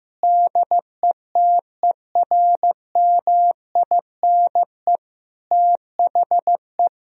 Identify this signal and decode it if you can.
Morse code, heard as a keyed tone, about 15 wpm, 700 Hz. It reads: DETERMINE THE
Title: Sentences from Top 1000 Words - Review Part 4 - 15wpm